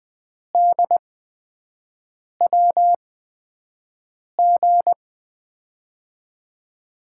Morse code, heard as a keyed tone, about 20 wpm, spaced as for 5 wpm, 700 Hz. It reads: DWG
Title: USA Call Signs - 6 Characters - 8x Character Spacing - 20wpm